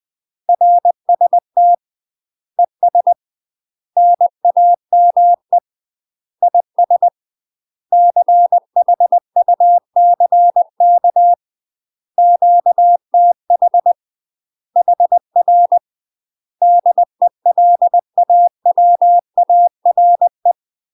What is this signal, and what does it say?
Morse code, heard as a keyed tone, about 20 wpm, 700 Hz. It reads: RST ES NAME IS CHUCK QTH HR DELAWARE
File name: UK MASTER QSOs TRAINING / 20WPM / UK MASTER QSOs 0050